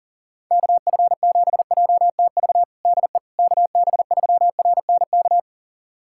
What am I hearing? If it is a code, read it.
Morse code, heard as a keyed tone, about 40 wpm, 700 Hz. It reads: KF7JTV DE KB3RNK
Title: USA Callsign DE Callsign 40wpm